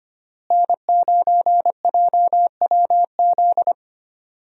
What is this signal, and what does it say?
Morse code, heard as a keyed tone, about 25 wpm, 700 Hz. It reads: N9JWZ
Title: USA Call Signs - 5 Characters - Repeated 2x - 25wpm